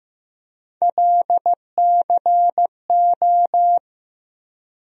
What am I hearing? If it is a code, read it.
Morse code, heard as a keyed tone, about 15 wpm, 700 Hz. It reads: LCO
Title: USA Call Sign Postfixes - 3 Letters 15wpm